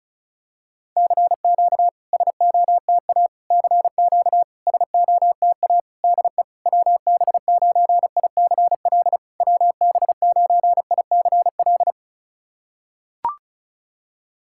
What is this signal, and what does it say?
Morse code, heard as a keyed tone, about 36 wpm, 700 Hz. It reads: CQ SOTA CQ SOTA DE WB9ICL WB9ICL
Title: SOTA QSO Practice 35wpm